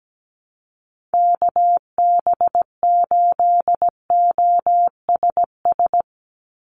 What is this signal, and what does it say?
Morse code, heard as a keyed tone, about 17 wpm, 700 Hz. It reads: KB8OSS
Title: USA Call Signs 17wpm